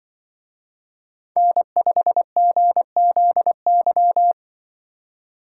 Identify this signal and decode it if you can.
Morse code, heard as a keyed tone, about 24 wpm, 700 Hz. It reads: N5GZY
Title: Speed Racing - USA 1x3 Call Signs 20wpm